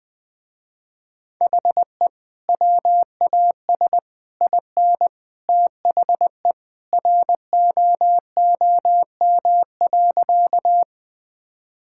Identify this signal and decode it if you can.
Morse code, heard as a keyed tone, about 21 wpm, 700 Hz. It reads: HE WAS IN THE ROOM.
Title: Sentences with 2-Word Phrases - Spoken Beginning - 20wpm